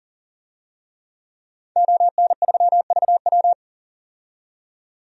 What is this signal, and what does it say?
Morse code, heard as a keyed tone, about 40 wpm, 700 Hz. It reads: ON3VW
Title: DXCC-100 Call Signs - 5 Characters - 40wpm